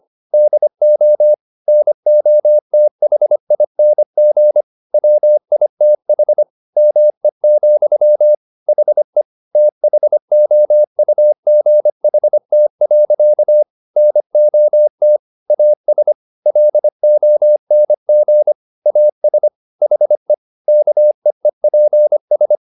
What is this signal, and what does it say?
Morse code, heard as a keyed tone, about 25 wpm, 600 Hz. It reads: DO NOTHING WITH ME, HE THOUGHT. NOT AS LONG AS HE KEEPS
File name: the old man and the sea part 1 25wpm